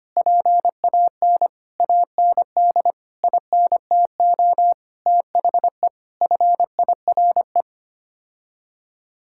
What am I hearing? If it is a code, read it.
Morse code, heard as a keyed tone, about 25 wpm, 700 Hz. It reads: PAN AND INTO THE FIRE
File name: Sets of 9 Words - 25wpm